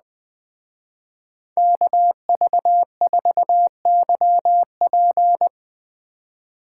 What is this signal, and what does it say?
Morse code, heard as a keyed tone, about 20 wpm, 700 Hz. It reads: KV4YP